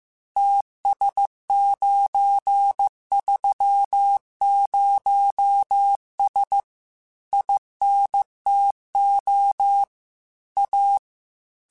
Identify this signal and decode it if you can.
Morse code, heard as a keyed tone, about 15 wpm, 800 Hz. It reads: TS930S INTO A